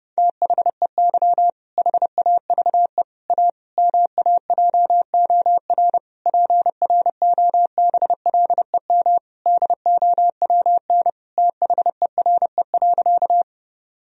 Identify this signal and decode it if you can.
Morse code, heard as a keyed tone, about 30 wpm, 700 Hz. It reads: THEY HAVE A MAJOR PROBLEM DOWN THERE.
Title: Sentences from Top 1000 Words - Review Part 4 - 30wpm